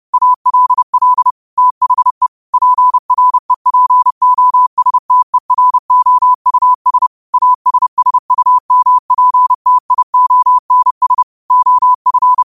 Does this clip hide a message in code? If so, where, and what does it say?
Yes: Morse code from 0.1 to 12.4 s, heard as a keyed tone, about 30 wpm, 1000 Hz. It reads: ALL THE PREPOSTEROUS ASSUMPTIONS OF